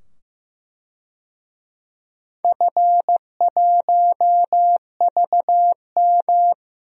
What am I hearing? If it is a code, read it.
Morse code, heard as a keyed tone, about 15 wpm, 700 Hz. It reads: F1VM
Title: DXCC-100 Call Signs - 4 Characters - 15wpm